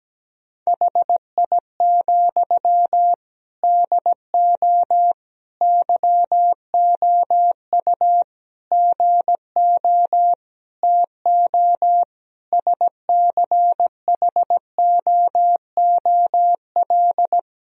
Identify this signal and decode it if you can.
Morse code, heard as a keyed tone, about 17 wpm, 700 Hz. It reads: HI, DO YOU GO TO SCHOOL
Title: Sentences from Top 1000 Words - Review Part 3 - 17wpm